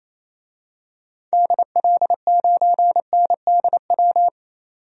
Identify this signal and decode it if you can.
Morse code, heard as a keyed tone, about 28 wpm, 700 Hz. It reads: DL9NDW
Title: DXCC-100 Call Signs - 6 Characters - 28wpm